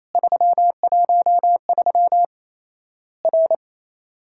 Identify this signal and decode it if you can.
Morse code, heard as a keyed tone, about 28 wpm, 700 Hz. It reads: 313 R
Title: Worked All Europe DX Contest - QTC Exchange - 28wpm